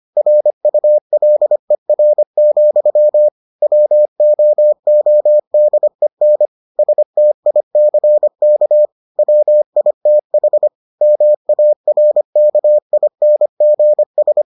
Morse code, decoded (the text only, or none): RULER, WOODEN STICK WITH MARKINGS